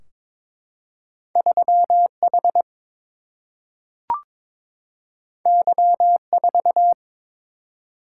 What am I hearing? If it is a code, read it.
Morse code, heard as a keyed tone, about 22 wpm, 700 Hz. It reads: 3 H Y 4
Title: Single-Letter Single-Number 22wpm